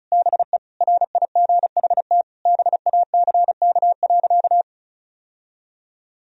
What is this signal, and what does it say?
Morse code, heard as a keyed tone, about 36 wpm, 700 Hz. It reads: BE RIGHT BACK.
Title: Sentences from Top 1000 Words - Review Part 4 - 35wpm